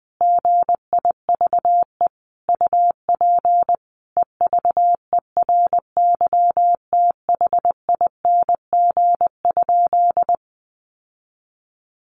Morse code, heard as a keyed tone, about 19 wpm, 700 Hz. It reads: GIVE UP EVERYTHING?
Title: Sentences from Top 700 Words 20wpm